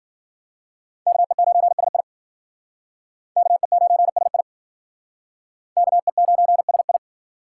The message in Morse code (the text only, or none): KE9SI KE9SI KE9SI